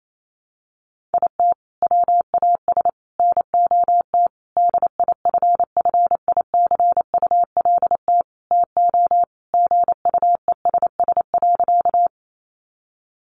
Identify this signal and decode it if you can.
Morse code, heard as a keyed tone, about 28 wpm, 700 Hz. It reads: IT WAS NOT DIFFICULT TO GUESS.
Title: Sentences from Top 1500 Words 28wpm